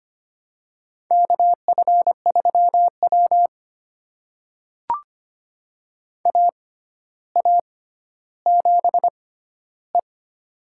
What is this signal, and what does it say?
Morse code, heard as a keyed tone, about 25 wpm, spaced as for 13 wpm, 700 Hz. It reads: KF3W AA7E
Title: USA Call Signs - 4 Characters - 6x Character Spacing - 25wpm